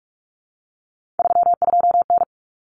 Morse code, heard as a keyed tone, about 44 wpm, 700 Hz. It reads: 32N
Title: Common Cut Numbers - 3-Digit 45wpm